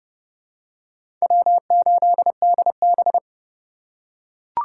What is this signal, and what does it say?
Morse code, heard as a keyed tone, about 30 wpm, 700 Hz. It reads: W8DB
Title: USA Call Signs - 2x Character Spacing - 30wpm